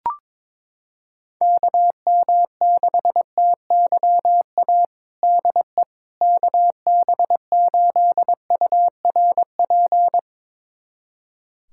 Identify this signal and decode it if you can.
Morse code, heard as a keyed tone, about 22 wpm, 700 Hz. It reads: KM6TYA DE KB8URP
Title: USA Callsign DE Callsign 22wpm